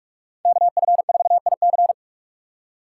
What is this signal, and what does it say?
Morse code, heard as a keyed tone, about 42 wpm, 700 Hz. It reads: KF4IC
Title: USA Call Signs - 5 Characters - 6x Character Spacing - 45wpm